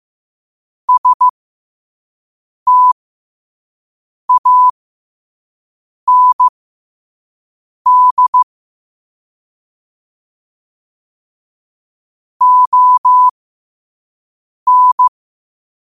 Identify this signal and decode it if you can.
Morse code, heard as a keyed tone, about 15 wpm, spaced as for 5 wpm, 1000 Hz. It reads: STAND ON
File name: qotdfiles / 5wpm